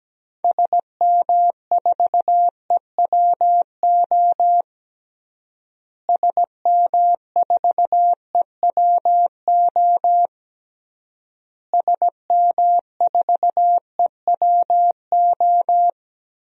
Morse code, heard as a keyed tone, about 17 wpm, 700 Hz. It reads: SM4EWO SM4EWO SM4EWO
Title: DXCC-100 Call Signs - 6 Characters - Repeated 3x - 17wpm